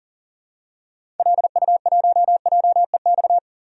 Morse code, heard as a keyed tone, about 40 wpm, 700 Hz. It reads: LU1JEX
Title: DXCC-100 Call Signs - 6 Characters - 40wpm